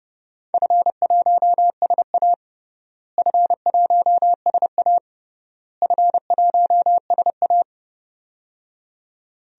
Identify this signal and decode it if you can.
Morse code, heard as a keyed tone, about 29 wpm, 700 Hz. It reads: F1SA F1SA F1SA
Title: DXCC-100 Call Signs - 4 Characters - Repeated 3x - 30wpm